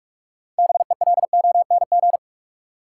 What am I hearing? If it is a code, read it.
Morse code, heard as a keyed tone, about 46 wpm, 700 Hz. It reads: BELONG
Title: Six Letter Words - 45wpm